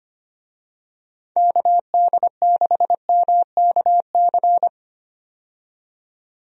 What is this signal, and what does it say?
Morse code, heard as a keyed tone, about 25 wpm, 700 Hz. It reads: KD6MKC
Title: USA Call Signs - 6 Characters - 25wpm